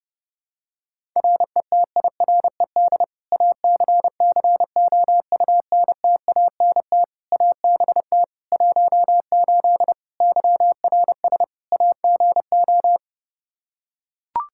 Morse code, heard as a keyed tone, about 30 wpm, 700 Hz. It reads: RETIRED ACCOUNTANT ABT 18 YRS AGO
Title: K1USN SST QSO Practice 30wpm